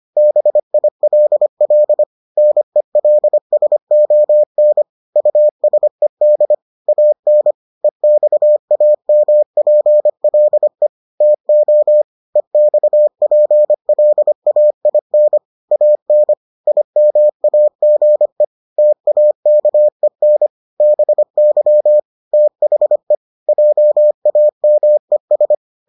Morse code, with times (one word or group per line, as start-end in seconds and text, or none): 0.2-2.0 BILL
2.4-4.8 NELSON
5.2-6.5 USED
6.9-7.5 AN
7.8-10.9 EXAMPLE
11.2-12.0 TO
12.4-15.4 EXPLAIN
15.7-16.3 AN
16.7-18.4 IMAGE
18.8-20.5 TAKEN
20.8-22.0 BY
22.3-23.1 THE
23.5-25.5 JAMES